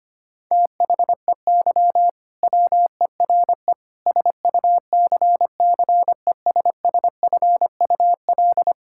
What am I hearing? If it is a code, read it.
Morse code, heard as a keyed tone, about 25 wpm, 700 Hz. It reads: THEY WERE SUCCESSFUL
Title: Sentences from Top 1000 Words - Review Part 6 - 25wpm